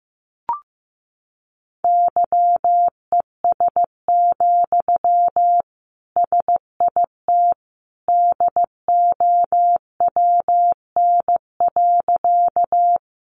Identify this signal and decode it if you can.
Morse code, heard as a keyed tone, about 15 wpm, 700 Hz. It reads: YES, SIT DOWN.
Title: Sentences from Top 500 Words 15wpm